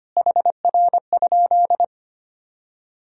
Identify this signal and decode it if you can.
Morse code, heard as a keyed tone, about 25 wpm, 700 Hz. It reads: HR?
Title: Sentences from Top 500 Words and Abbreviations 25wpm